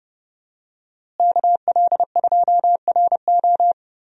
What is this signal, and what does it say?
Morse code, heard as a keyed tone, about 30 wpm, 700 Hz. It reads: KL2RO
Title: USA Call Signs - 5 Characters - Repeated 3x - 30wpm